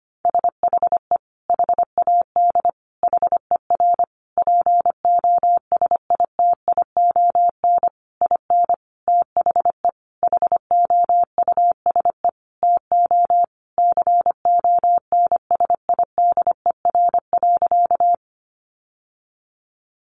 Morse code, heard as a keyed tone, about 25 wpm, 700 Hz. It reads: SHE HAD HER POSITION IN THE HOUSE TO CONSIDER.
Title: Sentences from Top 900 Words 25wpm